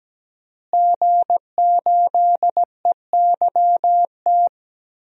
Morse code, heard as a keyed tone, about 17 wpm, 700 Hz. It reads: G8EYT
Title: DXCC-100 Call Signs - 5 Characters - Repeated 3x - 17wpm